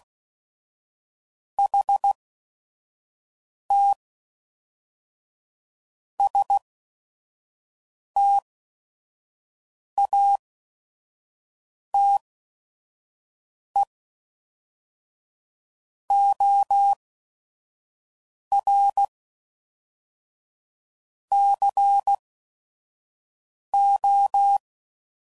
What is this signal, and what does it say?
Morse code, heard as a keyed tone, about 16 wpm, spaced as for 5 wpm, 800 Hz. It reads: HT STATE OR CO
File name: CW Code Groups @5WPM